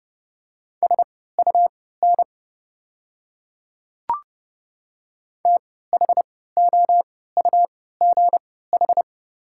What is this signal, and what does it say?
Morse code, heard as a keyed tone, about 30 wpm, spaced as for 17 wpm, 690 Hz. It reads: SUN THOUGH
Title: 3x Character Spacing - Top 500 Words - 30wpm